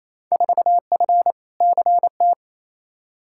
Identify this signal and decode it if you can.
Morse code, heard as a keyed tone, about 28 wpm, 700 Hz. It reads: 4F CT
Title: Field Day Exchanges Encore - 28wpm